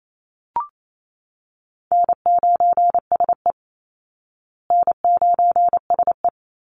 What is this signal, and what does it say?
Morse code, heard as a keyed tone, about 28 wpm, 700 Hz. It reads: N9SE N9SE
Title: Easy Call Signs Repeated 2x - 28wpm